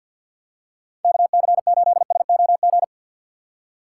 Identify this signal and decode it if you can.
Morse code, heard as a keyed tone, about 54 wpm, 700 Hz. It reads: K C 8 S O G
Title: USA Call Signs - 6 Characters - 8x Character Spacing - 50wpm